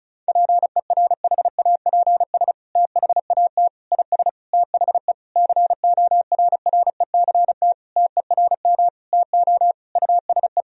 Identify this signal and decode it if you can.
Morse code, heard as a keyed tone, about 36 wpm, 700 Hz. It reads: PERHAPS THAT IS THE CORRECT TERM TO USE
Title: Sentences from Top 1000 Words - Review Part 4 - 35wpm